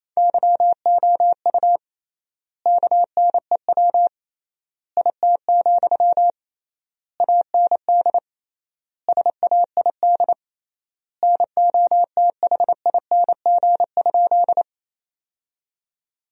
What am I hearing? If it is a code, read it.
Morse code, heard as a keyed tone, about 28 wpm, 700 Hz. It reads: YOU KNEW IT, AND SAID NOTHING?
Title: 3x Word-Spacing - Sentences from Top 500 Words - 28wpm